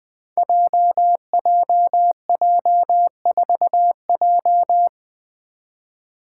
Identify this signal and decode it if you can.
Morse code, heard as a keyed tone, about 20 wpm, 700 Hz. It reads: JJJ4J
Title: Five Challenging Letter-Number Combinations 20wpm